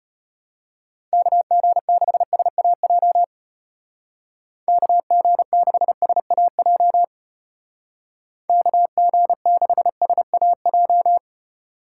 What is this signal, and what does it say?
Morse code, heard as a keyed tone, about 30 wpm, 700 Hz. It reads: KG6SAJ KG6SAJ KG6SAJ
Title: Speed Racing - USA 2x3 Call Signs 25wpm